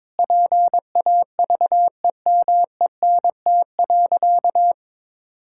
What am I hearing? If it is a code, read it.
Morse code, heard as a keyed tone, about 22 wpm, 700 Hz. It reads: PAVEMENT.
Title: Pangram Challenge v2 - 22wpm